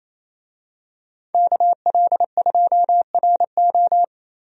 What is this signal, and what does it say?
Morse code, heard as a keyed tone, about 28 wpm, 700 Hz. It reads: KL2RO
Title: USA Call Signs - 5 Characters - Repeated 3x - 28wpm